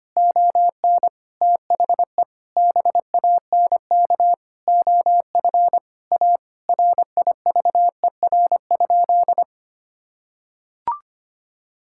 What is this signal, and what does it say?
Morse code, heard as a keyed tone, about 25 wpm, 700 Hz. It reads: ON THE BANK OF A RIVER?
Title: Sentences from Top 1000 Words - Review Part 6 - 25wpm